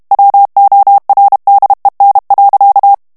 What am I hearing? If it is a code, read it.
Morse code, heard as a keyed tone, about 32 wpm, 800 Hz. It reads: WORDEN.
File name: HOLLANDAIS texte clair 2 vitesse 30 mots minute